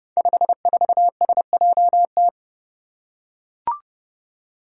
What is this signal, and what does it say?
Morse code, heard as a keyed tone, about 30 wpm, 700 Hz. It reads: 54SJT
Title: Five Challenging Letter-Number Combinations 30wpm